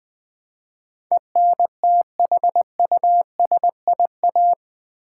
Morse code, heard as a keyed tone, about 20 wpm, 700 Hz. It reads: ENTHUSIA
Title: Build A Word - 4 Letters At A Time - 12-Letter Words - 20wpm